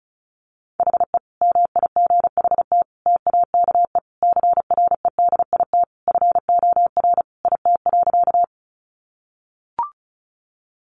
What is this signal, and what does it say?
Morse code, heard as a keyed tone, about 34 wpm, 700 Hz. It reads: HE MIGHT TAKE CREDIT FOR IT.
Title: Sentences from Top 1000 Words 35wpm